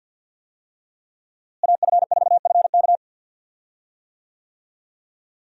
Morse code, heard as a keyed tone, about 49 wpm, 700 Hz. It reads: AF4FK
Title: USA Call Signs - 5 Characters - Repeated 3x - 50wpm